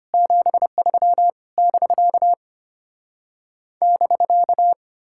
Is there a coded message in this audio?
Yes: Morse code from 0.1 to 4.7 s, heard as a keyed tone, about 33 wpm, 700 Hz. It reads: 73 <BK> <BK>